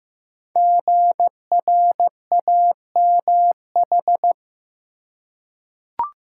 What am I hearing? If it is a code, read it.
Morse code, heard as a keyed tone, about 15 wpm, 700 Hz. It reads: GRAMH
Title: Code Groups - 5 Letters - 15wpm